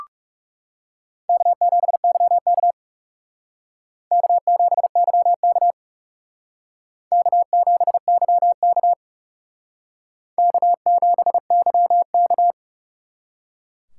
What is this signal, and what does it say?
Morse code, heard as a keyed tone, about 37 wpm, 700 Hz. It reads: K7YK K7YK K7YK K7YK
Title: Speed Racing - USA 1x2 Call Signs 30wpm